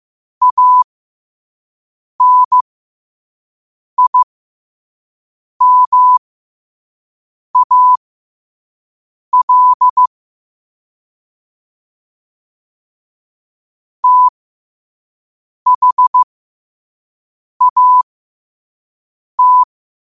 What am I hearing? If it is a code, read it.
Morse code, heard as a keyed tone, about 15 wpm, spaced as for 5 wpm, 1000 Hz. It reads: ANIMAL THAT
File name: 2024-10-07-5wpm